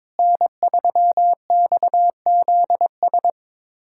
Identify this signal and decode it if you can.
Morse code, heard as a keyed tone, about 22 wpm, 700 Hz. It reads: N3XZS
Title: USA Callsign DE Callsign 22wpm